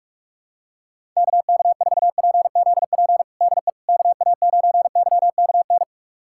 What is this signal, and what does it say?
Morse code, heard as a keyed tone, about 45 wpm, 700 Hz. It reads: KK4PZP DE KA9YKN
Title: USA Callsign DE Callsign - 6 Characters - 45wpm